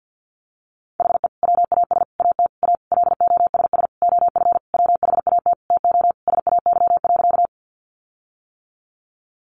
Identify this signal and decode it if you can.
Morse code, heard as a keyed tone, about 50 wpm, 700 Hz. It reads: HE WAS AT A LOSS OF WHAT TO SAY.
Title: Sentences from Top 1400 Words 50wpm